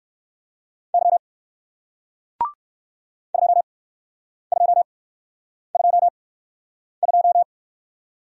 Morse code, heard as a keyed tone, about 60 wpm, spaced as for 9 wpm, 700 Hz. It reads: Y 2222